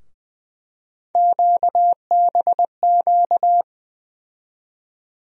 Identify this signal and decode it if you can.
Morse code, heard as a keyed tone, about 20 wpm, 700 Hz. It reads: QBQ
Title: USA Call Sign Postfixes - 3 Letters 20wpm